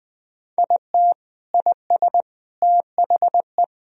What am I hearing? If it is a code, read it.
Morse code, heard as a keyed tone, about 21 wpm, 700 Hz. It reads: IT IS THE
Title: Sentences from Top 1000 Words - Review Part 2 - 20wpm